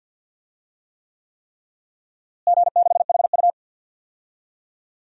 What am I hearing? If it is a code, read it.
Morse code, heard as a keyed tone, about 47 wpm, 700 Hz. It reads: G6HU
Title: DXCC-100 Call Signs - 4 Characters - Repeated 3x - 50wpm